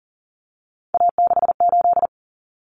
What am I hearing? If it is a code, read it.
Morse code, heard as a keyed tone, about 40 wpm, 700 Hz. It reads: A68
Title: Common Cut Numbers - 3-Digit 40wpm